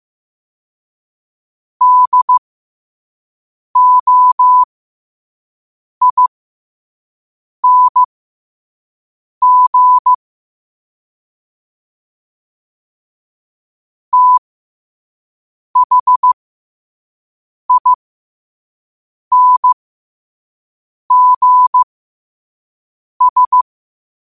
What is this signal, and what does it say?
Morse code, heard as a keyed tone, about 15 wpm, spaced as for 5 wpm, 1000 Hz. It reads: DOING THINGS
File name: qotdfiles / 5wpm